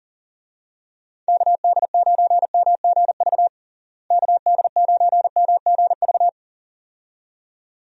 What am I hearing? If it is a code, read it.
Morse code, heard as a keyed tone, about 40 wpm, 700 Hz. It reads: KD9MGV KD9MGV
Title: USA Call Signs - 6 Characters - Repeated 2x - 40wpm